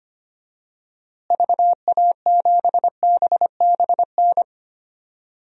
Morse code, heard as a keyed tone, about 25 wpm, 700 Hz. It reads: VA7BBN
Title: DXCC-100 Call Signs - 6 Characters - Repeated 3x - 25wpm